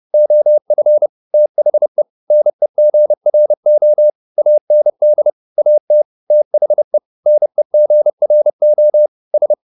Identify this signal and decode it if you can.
Morse code, heard as a keyed tone, about 30 wpm, 600 Hz. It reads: OF THE NEGRO AND AT THE NEGRO S
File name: the old man and the sea 2 30wpm